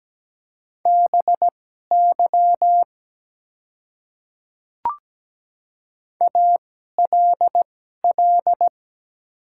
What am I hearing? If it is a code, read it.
Morse code, heard as a keyed tone, about 17 wpm, spaced as for 12 wpm, 690 Hz. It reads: BY ALL
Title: 2x Character Spacing - Top 100 Words - 17wpm